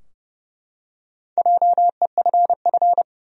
Morse code, heard as a keyed tone, about 30 wpm, 700 Hz. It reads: JEFF
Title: USA Common Names - Max 4 Letters - 30wpm